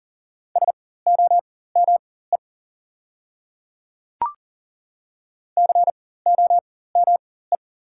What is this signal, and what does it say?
Morse code, heard as a keyed tone, about 40 wpm, spaced as for 19 wpm, 700 Hz. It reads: SOME COME
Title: 4x Character Spacing - Top 100 Words - 40wpm